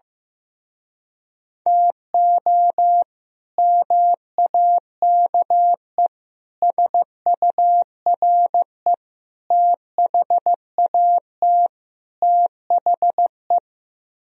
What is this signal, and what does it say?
Morse code, heard as a keyed tone, about 15 wpm, 700 Hz. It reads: TO MAKE SURE THAT THE
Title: Speed-Racing - Sets of 5 Words 15wpm